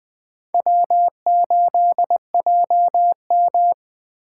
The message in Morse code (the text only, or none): W8JM